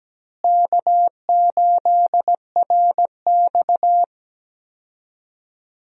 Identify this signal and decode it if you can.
Morse code, heard as a keyed tone, about 17 wpm, 700 Hz. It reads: K8RX
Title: USA Call Signs - 4 Characters - Repeated 2x - 17wpm